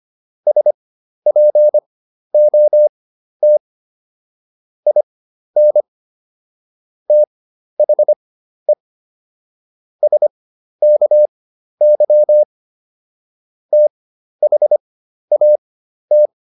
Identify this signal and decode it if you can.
Morse code, heard as a keyed tone, about 25 wpm, spaced as for 12 wpm, 600 Hz. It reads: SPOT IN THE SKY THAT